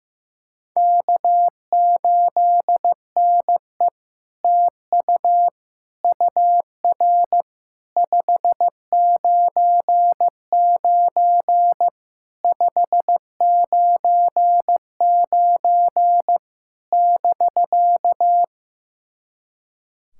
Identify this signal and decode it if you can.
Morse code, heard as a keyed tone, about 15 wpm, 700 Hz. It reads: K8NE TU UR 599 599 <BK>
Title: SOTA QSO Practice 15wpm